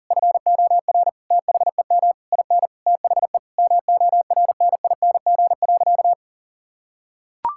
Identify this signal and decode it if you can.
Morse code, heard as a keyed tone, about 40 wpm, 700 Hz. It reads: FOR THEM IN THE MORNING.
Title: Sentences from Top 1000 Words - Review Part 5 - 40wpm